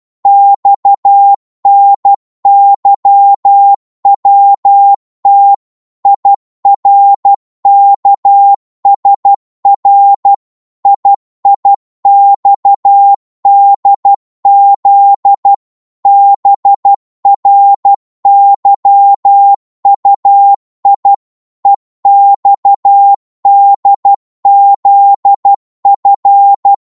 Morse code, heard as a keyed tone, about 12 wpm, 800 Hz. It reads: XNYWT IRKSR IIXDZ BRYUI EXDZF